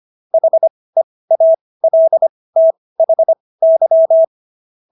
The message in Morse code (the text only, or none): HEALTHY